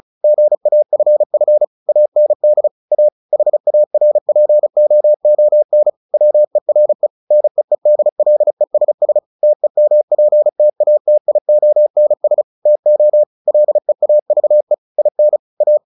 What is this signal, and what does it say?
Morse code, heard as a keyed tone, about 34 wpm, 600 Hz. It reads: GAFF AND A HARPOON WERE NEEDLESS TEMPTATIONS TO LEAVE IN A
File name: the old man and the sea / the old man and the sea 1 35wpm